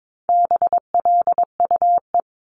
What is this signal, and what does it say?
Morse code, heard as a keyed tone, about 22 wpm, 700 Hz. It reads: BLUE
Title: Multi-letter Phonemes 22wpm